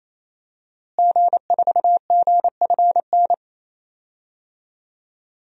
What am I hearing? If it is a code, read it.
Morse code, heard as a keyed tone, about 27 wpm, 700 Hz. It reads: G4GFN
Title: DXCC-100 Call Signs - 5 Characters - Repeated 3x - 28wpm